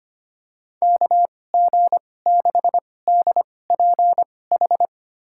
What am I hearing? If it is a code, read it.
Morse code, heard as a keyed tone, about 25 wpm, spaced as for 18 wpm, 700 Hz. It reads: KG6DPH
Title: USA Call Signs - 6 Characters - 2x Character Spacing - 25wpm